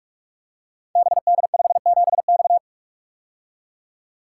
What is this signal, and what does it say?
Morse code, heard as a keyed tone, about 44 wpm, 700 Hz. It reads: BD57X
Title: Code Groups - 5 Letter:Numbers - 45wpm